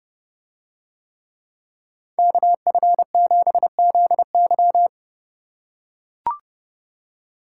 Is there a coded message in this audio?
Yes: Morse code from 2.2 to 4.9 s, heard as a keyed tone, about 30 wpm, 700 Hz. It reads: KF7ZY